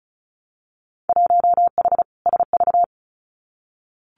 Morse code, heard as a keyed tone, about 36 wpm, 700 Hz. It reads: 1H SV
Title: Winter Field Day Exchanges 35wpm